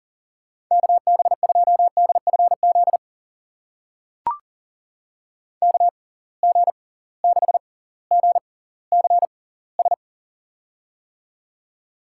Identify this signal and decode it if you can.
Morse code, heard as a keyed tone, about 40 wpm, spaced as for 20 wpm, 700 Hz. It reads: KB2DFZ KG6GCS
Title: USA Call Signs - 6 Characters - 6x Character Spacing - 40wpm